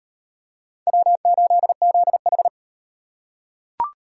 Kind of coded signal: Morse code

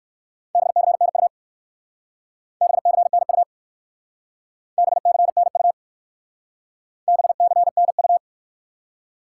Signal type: Morse code

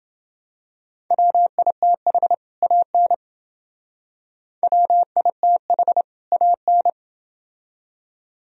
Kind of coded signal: Morse code

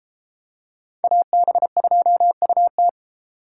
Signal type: Morse code

